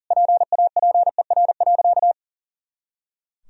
Morse code, heard as a keyed tone, about 40 wpm, 700 Hz. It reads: PAPER.
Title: Sentences from Top 300 Words and Abbreviations 40wpm